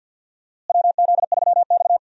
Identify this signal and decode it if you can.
Morse code, heard as a keyed tone, about 53 wpm, 700 Hz. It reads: WZ3X
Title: USA Call Signs - 4 Characters - Repeated 2x - 50wpm